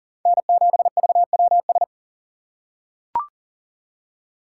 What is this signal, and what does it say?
Morse code, heard as a keyed tone, about 40 wpm, 700 Hz. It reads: N7VWS
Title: USA Call Signs - 5 Characters - 3x Character Spacing - 40wpm